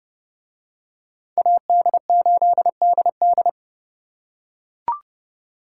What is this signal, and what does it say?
Morse code, heard as a keyed tone, about 30 wpm, 700 Hz. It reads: AD8DD